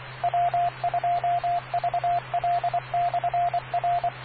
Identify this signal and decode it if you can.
Morse code, heard as a keyed tone, about 24 wpm, 700 Hz, with radio noise. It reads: W2VL/R